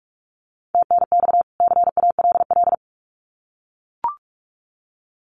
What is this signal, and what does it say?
Morse code, heard as a keyed tone, about 44 wpm, 700 Hz. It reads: TNX CALL
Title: Common QSO Phrases 45wpm